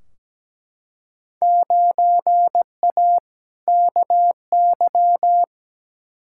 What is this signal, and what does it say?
Morse code, heard as a keyed tone, about 17 wpm, 700 Hz. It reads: 9A KY
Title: Field Day Exchanges Encore - 17wpm